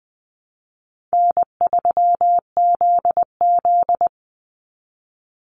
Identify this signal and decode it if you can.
Morse code, heard as a keyed tone, about 20 wpm, 700 Hz. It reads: N3ZZ
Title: Call Signs Repeated 3x - 20wpm